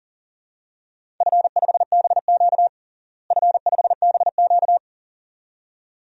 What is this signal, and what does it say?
Morse code, heard as a keyed tone, about 40 wpm, 700 Hz. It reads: F5BQ F5BQ
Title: DXCC-100 Call Signs - 4 Characters - Repeated 2x - 40wpm